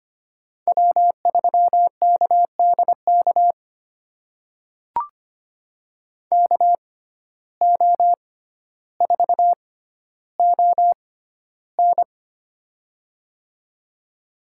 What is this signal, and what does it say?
Morse code, heard as a keyed tone, about 25 wpm, spaced as for 13 wpm, 700 Hz. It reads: W3KDK KO4ON
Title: USA Call Signs - 5 Characters - 6x Character Spacing - 25wpm